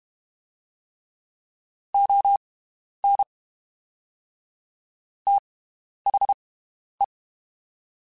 Morse code, heard as a keyed tone, about 32 wpm, spaced as for 11 wpm, 800 Hz. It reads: ON THE
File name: (spaced) / QSOs Spaced-mp3s / WD0FBT 1 @30 sp